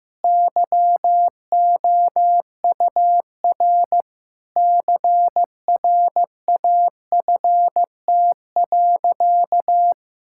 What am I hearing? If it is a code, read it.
Morse code, heard as a keyed tone, about 15 wpm, 700 Hz. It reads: YOUR CRAFT.
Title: Sentences from Top 1000 Words - Review Part 6 - 15wpm